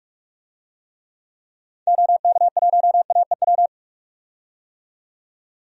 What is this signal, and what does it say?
Morse code, heard as a keyed tone, about 45 wpm, 700 Hz. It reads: OK1AEW
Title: DXCC-100 Call Signs - 6 Characters - 45wpm